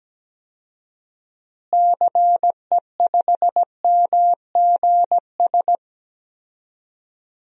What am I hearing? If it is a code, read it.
Morse code, heard as a keyed tone, about 17 wpm, 700 Hz. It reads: CE5MGS
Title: DXCC-100 Call Signs - 6 Characters - Repeated 3x - 17wpm